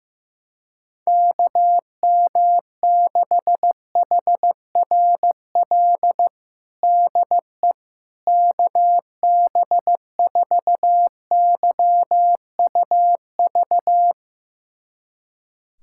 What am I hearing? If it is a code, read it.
Morse code, heard as a keyed tone, about 15 wpm, 700 Hz. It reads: KM6HRL DE KB4YUV
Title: USA Callsign DE Callsign 15wpm